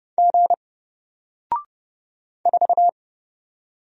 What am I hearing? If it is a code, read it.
Morse code, heard as a keyed tone, about 30 wpm, 700 Hz. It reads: G 4
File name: Single-Letter-Number-30wpm